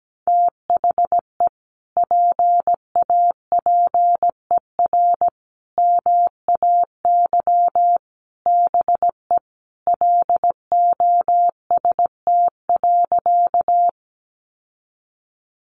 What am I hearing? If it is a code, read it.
Morse code, heard as a keyed tone, about 17 wpm, 700 Hz. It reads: THE PAPER MAY BE LOST.
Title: Sentences from Top 1400 Words 17wpm